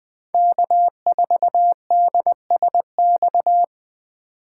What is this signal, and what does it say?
Morse code, heard as a keyed tone, about 20 wpm, 700 Hz. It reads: K4DSX
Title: 2023 POTA Activator Call Signs - 20wpm